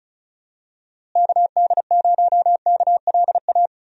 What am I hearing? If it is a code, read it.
Morse code, heard as a keyed tone, about 36 wpm, 700 Hz. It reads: KD0KLA
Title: USA Call Signs - 6 Characters - Repeated 3x - 35wpm